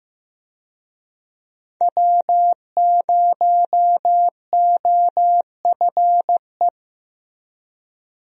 Morse code, heard as a keyed tone, about 15 wpm, 700 Hz. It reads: W0OFE
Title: USA Call Signs - 5 Characters - Repeated 3x - 15wpm